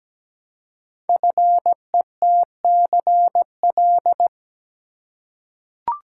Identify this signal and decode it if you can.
Morse code, heard as a keyed tone, about 17 wpm, 700 Hz. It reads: FETCL
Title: Code Groups - 5 Letters - 17wpm